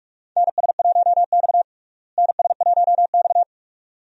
Morse code, heard as a keyed tone, about 46 wpm, 700 Hz. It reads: NS1X NS1X
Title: USA Call Signs - 4 Characters - Repeated 2x - 45wpm